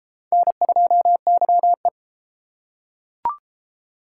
Morse code, heard as a keyed tone, about 30 wpm, 700 Hz. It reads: N2YE